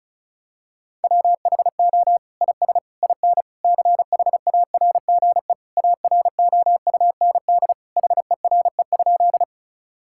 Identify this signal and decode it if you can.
Morse code, heard as a keyed tone, about 34 wpm, 700 Hz. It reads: WHO IS IN CHARGE AROUND HERE?
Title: Sentences from Top 1000 Words - Review Part 6 - 35wpm